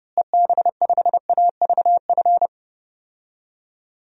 Morse code, heard as a keyed tone, about 30 wpm, 700 Hz. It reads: EB5AVF
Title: DXCC-100 Call Signs - 6 Characters - Repeated 2x - 30wpm